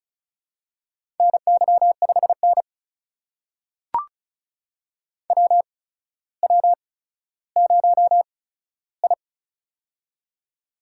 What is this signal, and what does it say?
Morse code, heard as a keyed tone, about 34 wpm, spaced as for 15 wpm, 700 Hz. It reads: NY5N WW0I
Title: USA Call Signs - 4 Characters - 8x Character Spacing - 35wpm